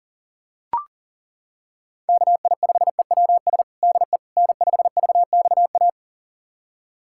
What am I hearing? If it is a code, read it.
Morse code, heard as a keyed tone, about 40 wpm, 700 Hz. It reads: KI5EWS DE N5VXA
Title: USA Callsign DE Callsign 40wpm